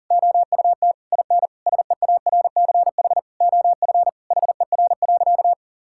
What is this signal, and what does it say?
Morse code, heard as a keyed tone, about 37 wpm, 700 Hz. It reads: OUT IN SEARCH OF HER.
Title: Sentences from Top 1000 Words - Review Part 5 - 40wpm